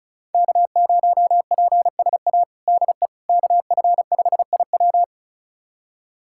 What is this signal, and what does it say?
Morse code, heard as a keyed tone, about 34 wpm, 700 Hz. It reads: K0PSA DE KF5IW
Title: USA Callsign DE Callsign - 5 Characters - 35wpm